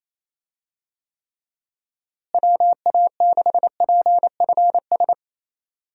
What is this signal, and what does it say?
Morse code, heard as a keyed tone, about 28 wpm, 700 Hz. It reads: WA6PFS
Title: USA Call Signs - 6 Characters - Repeated 2x - 28wpm